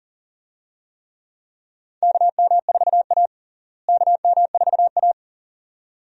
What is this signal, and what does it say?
Morse code, heard as a keyed tone, about 40 wpm, 700 Hz. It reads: KM4A KM4A
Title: USA Call Signs - 4 Characters - Repeated 2x - 40wpm